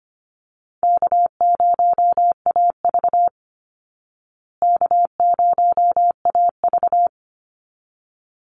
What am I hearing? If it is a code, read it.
Morse code, heard as a keyed tone, about 25 wpm, 700 Hz. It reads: K0AV K0AV
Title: Easy Call Signs Repeated 2x - 25wpm